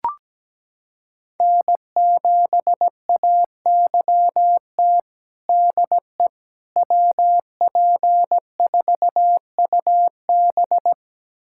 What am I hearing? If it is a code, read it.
Morse code, heard as a keyed tone, about 17 wpm, 700 Hz. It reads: N7AYT DE WP4UB
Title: USA Callsign DE Callsign - 5 Characters - 17wpm